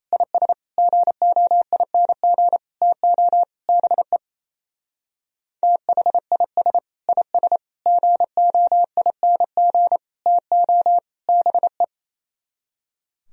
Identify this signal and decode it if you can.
Morse code, heard as a keyed tone, about 35 wpm, 700 Hz. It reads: IS GOING TO BE THIS IS GOING TO BE
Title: Speed-Racing - Sets of 5 Words 28wpm